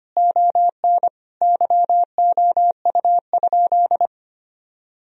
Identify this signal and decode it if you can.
Morse code, heard as a keyed tone, about 25 wpm, 700 Hz. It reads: ON YOU?
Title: Sentences from Top 1000 Words - Review Part 3 - 25wpm